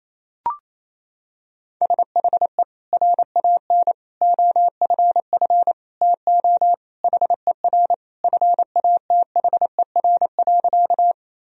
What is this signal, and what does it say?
Morse code, heard as a keyed tone, about 28 wpm, 700 Hz. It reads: SHE RAN OFF TO HER FATHER.
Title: Sentences from Top 1000 Words - Review Part 3 - 28wpm